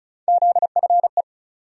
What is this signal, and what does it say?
Morse code, heard as a keyed tone, about 34 wpm, 700 Hz. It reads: ZFE
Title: USA Call Sign Postfixes - 3 Letters 35wpm